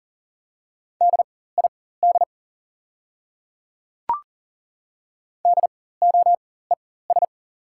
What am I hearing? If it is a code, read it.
Morse code, heard as a keyed tone, about 40 wpm, spaced as for 19 wpm, 700 Hz. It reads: DID DOES
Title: 4x Character Spacing - Top 500 Words - 40wpm